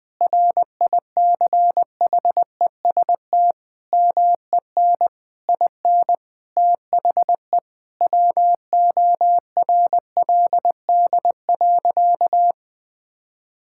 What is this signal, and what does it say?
Morse code, heard as a keyed tone, about 20 wpm, 700 Hz. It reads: RICHEST MEN IN THE WORLD.
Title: Sentences with 8-Word Phrases - Spoken Beginning - 20wpm